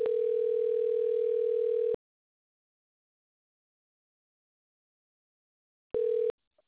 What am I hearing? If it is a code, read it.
Ringback tone (North America), 440 + 480 Hz.